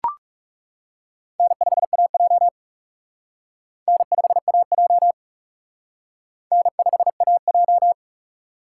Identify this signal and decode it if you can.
Morse code, heard as a keyed tone, about 36 wpm, 700 Hz. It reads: N5AJ N5AJ N5AJ
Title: Speed Racing - USA 1x2 Call Signs 30wpm